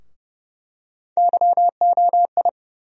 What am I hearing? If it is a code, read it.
Morse code, heard as a keyed tone, about 30 wpm, 700 Hz. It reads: YOI
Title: USA Call Sign Postfixes - 3 Letters 30wpm